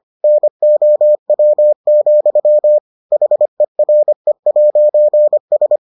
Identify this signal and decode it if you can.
Morse code, heard as a keyed tone, about 25 wpm, 600 Hz. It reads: NOW, HERE'S